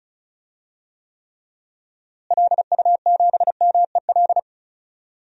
Morse code, heard as a keyed tone, about 34 wpm, 700 Hz. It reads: LU7MEL
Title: DXCC-100 Call Signs - 6 Characters - Repeated 3x - 35wpm